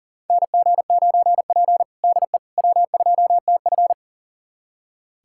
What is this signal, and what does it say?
Morse code, heard as a keyed tone, about 37 wpm, 700 Hz. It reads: NG9P DE W2TF
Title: USA Callsign DE Callsign - 4 Characters - 40wpm